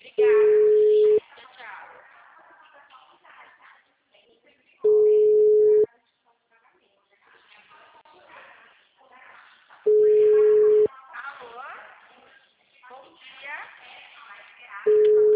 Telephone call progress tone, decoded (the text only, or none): ringback tone